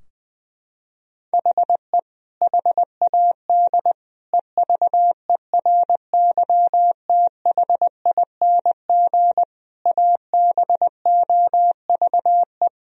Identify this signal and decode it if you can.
Morse code, heard as a keyed tone, about 20 wpm, 700 Hz. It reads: HE HAD EVERYTHING ABOVE
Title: Sentences from Top 1000 Words - Review Part 3 - 20wpm